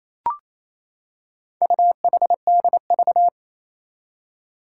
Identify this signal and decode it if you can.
Morse code, heard as a keyed tone, about 27 wpm, 700 Hz. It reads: UHDV
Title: Four Challenging Letter-Number Combinations 28wpm